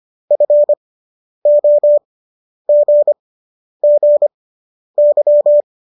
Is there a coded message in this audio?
Yes: Morse code from 0.3 to 5.6 s, heard as a keyed tone, about 25 wpm, spaced as for 10 wpm, 590 Hz. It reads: FOGGY